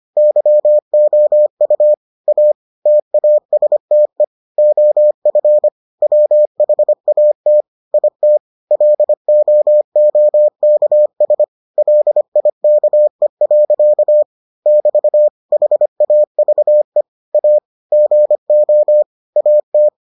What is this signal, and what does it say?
Morse code, heard as a keyed tone, about 25 wpm, 600 Hz. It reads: YOU A TASTE OF WHAT IT LOOKS LIKE. <BT> HAVE A GO AT